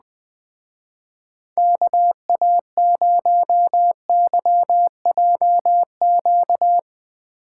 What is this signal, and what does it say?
Morse code, heard as a keyed tone, about 20 wpm, 700 Hz. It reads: KA0YJQ